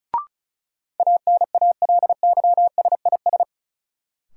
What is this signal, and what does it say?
Morse code, heard as a keyed tone, about 36 wpm, 700 Hz. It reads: ANALYSIS
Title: Sentences from Top 1400 Words - New Word Drill Alt - 35wpm